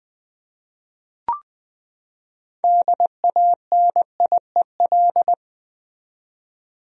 Morse code, heard as a keyed tone, about 21 wpm, 700 Hz. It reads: DANIEL
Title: USA Common Names - 20wpm